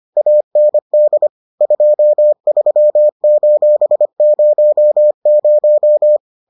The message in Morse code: AND 23:00

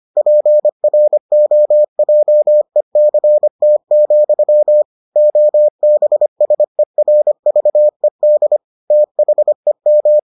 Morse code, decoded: PROJECT, OBSERVED THEM

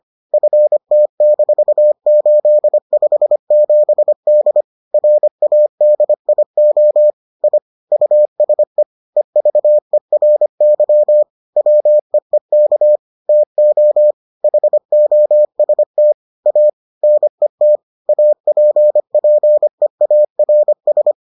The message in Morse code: FT-857D RADIO I USE EVERY WEEK TO HOST A NET APPEARS

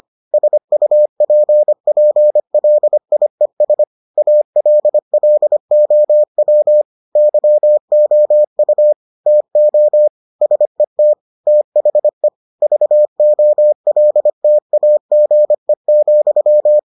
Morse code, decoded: SUPPLIES ALLOW YOU TO SET THE VOLTAGE,

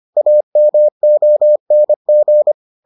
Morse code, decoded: AMONG

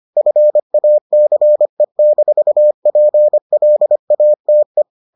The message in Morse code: FACE-PLATE

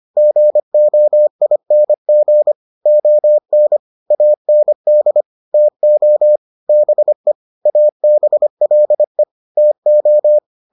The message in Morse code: GOING ON AND TO BE ABLE TO

